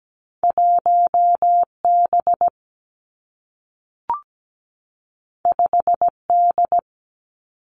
Morse code, heard as a keyed tone, about 17 wpm, 700 Hz. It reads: 1B 5D
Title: Two Challenging Letter-Number Combinations 17wpm